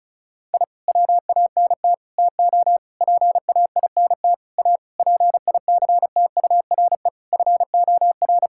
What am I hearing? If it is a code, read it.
Morse code, heard as a keyed tone, about 34 wpm, 700 Hz. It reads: I WANT TO PAINT A PICTURE FOR
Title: Sentences from Top 1000 Words - Review Part 6 - 35wpm